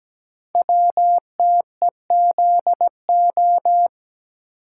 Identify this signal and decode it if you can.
Morse code, heard as a keyed tone, about 17 wpm, 700 Hz. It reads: WTEZO
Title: Code Groups - 5 Letters - 17wpm